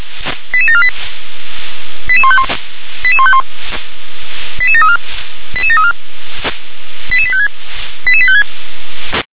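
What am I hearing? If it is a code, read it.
Five-tone selective call (ZVEI), heard as quick runs of five tones, tones of about 70 ms. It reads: 88748 88151 88151 88734 88734 88856 88856
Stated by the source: ZVEI five-tone call received on SDR.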